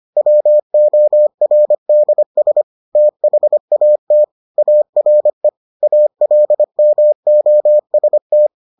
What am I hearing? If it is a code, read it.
Morse code, heard as a keyed tone, about 25 wpm, 600 Hz. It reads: WORDS THAT ARE ALMOST